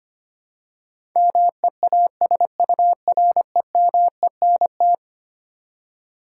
Morse code, heard as a keyed tone, about 25 wpm, 700 Hz. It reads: MEASUREMENT
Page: morse-code-ninja-practice-25wpm